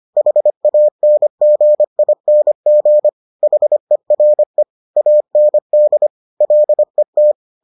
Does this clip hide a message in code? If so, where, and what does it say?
Yes: Morse code from 0.2 to 7.3 s, heard as a keyed tone, about 25 wpm, 600 Hz. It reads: HANGING HERE AND LET